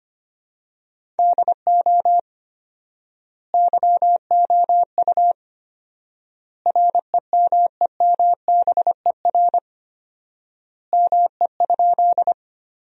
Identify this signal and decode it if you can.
Morse code, heard as a keyed tone, about 25 wpm, 700 Hz. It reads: DO YOU REMEMBER ME?
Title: 4x Word-Spacing - Sentences from Top 400 Words - 25wpm